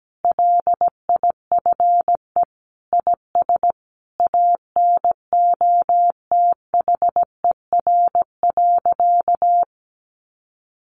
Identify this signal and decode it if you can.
Morse code, heard as a keyed tone, about 17 wpm, 700 Hz. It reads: LIFE IS ANOTHER.
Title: Sentences from Top 1000 Words 17wpm